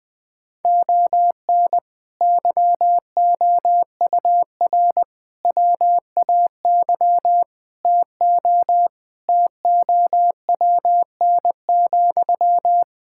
Morse code, heard as a keyed tone, about 20 wpm, 700 Hz. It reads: ON YOUR WAY TO TOWN,
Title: Sentences from Top 1000 Words - Review Part 6 - 20wpm